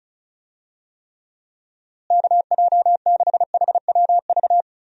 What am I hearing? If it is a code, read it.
Morse code, heard as a keyed tone, about 36 wpm, 700 Hz. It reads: KJ6HWV
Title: USA Call Signs - 6 Characters - Repeated 2x - 35wpm